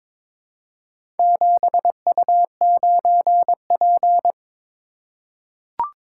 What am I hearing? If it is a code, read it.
Morse code, heard as a keyed tone, about 22 wpm, 700 Hz. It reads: 7U9P
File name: practice / 22wpm / Four Letter-Number Combinations - 22wpm